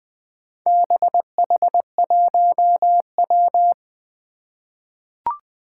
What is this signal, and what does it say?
Morse code, heard as a keyed tone, about 20 wpm, 700 Hz. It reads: BH1W
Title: Four Challenging Letter-Number Combinations 20wpm